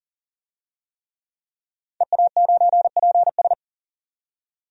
Morse code, heard as a keyed tone, about 40 wpm, 700 Hz. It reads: EA9PS
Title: DXCC-100 Call Signs - 5 Characters - 40wpm